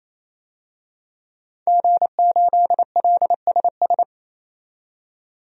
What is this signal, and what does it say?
Morse code, heard as a keyed tone, about 28 wpm, 700 Hz. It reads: G8LSS
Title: DXCC-100 Call Signs - 5 Characters - 28wpm